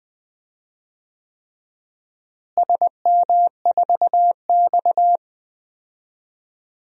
Morse code, heard as a keyed tone, about 20 wpm, 700 Hz. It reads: SM4X
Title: DXCC-100 Call Signs - 4 Characters - Repeated 3x - 20wpm